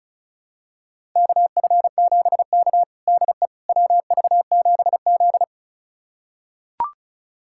Morse code, heard as a keyed tone, about 36 wpm, 700 Hz. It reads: KF7K DE WV7Z
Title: USA Callsign DE Callsign - 4 Characters - 35wpm